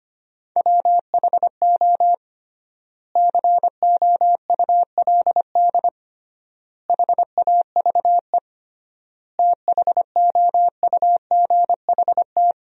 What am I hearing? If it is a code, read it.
Morse code, heard as a keyed tone, about 25 wpm, 700 Hz. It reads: WHO COULD HAVE THOUGHT
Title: 3x Word-Spacing - Sentences from Top 300 Words - 25wpm